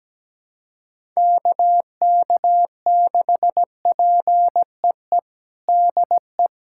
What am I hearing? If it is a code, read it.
Morse code, heard as a keyed tone, about 17 wpm, 700 Hz. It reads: KK6PEE DE
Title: USA Callsign DE Callsign - 6 Characters - 17wpm